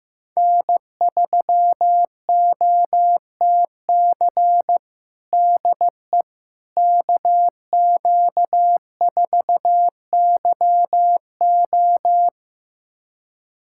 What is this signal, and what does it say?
Morse code, heard as a keyed tone, about 15 wpm, 700 Hz. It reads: N3OTC DE KQ4YO
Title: USA Callsign DE Callsign - 5 Characters - 15wpm